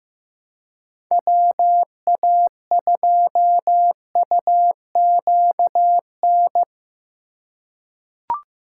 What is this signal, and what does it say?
Morse code, heard as a keyed tone, about 15 wpm, 700 Hz. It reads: WA2UQN
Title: USA Call Signs - 6 Characters - 2x Character Spacing - 15wpm